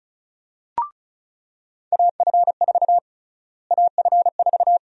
Morse code, heard as a keyed tone, about 36 wpm, 700 Hz. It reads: AF4 AF4
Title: USA Call Sign Prefixes - Repeated 2x - 35wpm